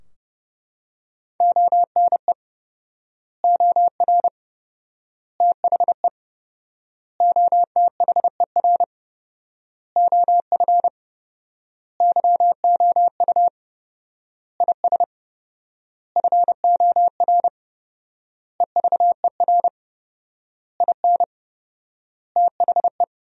Morse code, heard as a keyed tone, about 30 wpm, 700 Hz. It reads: ONE OR THE OTHER OF YOU IS FOR EVER IN THE
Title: 4x Word-Spacing - Sentences from Top 400 Words - 30wpm